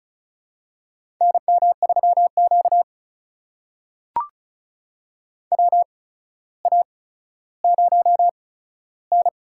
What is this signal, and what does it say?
Morse code, heard as a keyed tone, about 36 wpm, spaced as for 15 wpm, 700 Hz. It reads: NM3Q WA0N
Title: USA Call Signs - 4 Characters - 8x Character Spacing - 35wpm